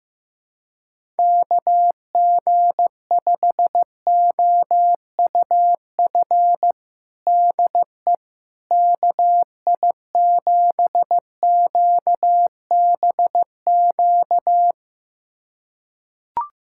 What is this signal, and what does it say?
Morse code, heard as a keyed tone, about 15 wpm, 700 Hz. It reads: KG5OUF DE KI7QBQ
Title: USA Callsign DE Callsign 15wpm